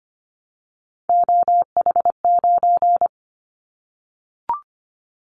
Morse code, heard as a keyed tone, about 25 wpm, 700 Hz. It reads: OH9
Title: Callsign Prefixes for Top 50 DXCC - 25wpm